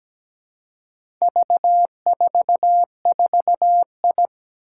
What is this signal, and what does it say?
Morse code, heard as a keyed tone, about 17 wpm, 700 Hz. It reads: V44I
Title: Four Challenging Letter-Number Combinations 17wpm